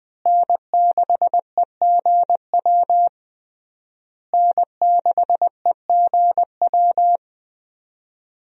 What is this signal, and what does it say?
Morse code, heard as a keyed tone, about 20 wpm, 700 Hz. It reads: N6EGW N6EGW
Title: USA Call Signs - 5 Characters - Repeated 3x - 20wpm